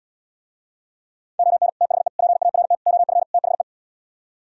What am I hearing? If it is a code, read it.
Morse code, heard as a keyed tone, about 77 wpm, 700 Hz. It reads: ON THE PART OF THE